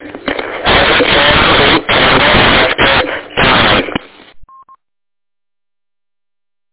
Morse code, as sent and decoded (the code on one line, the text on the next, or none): none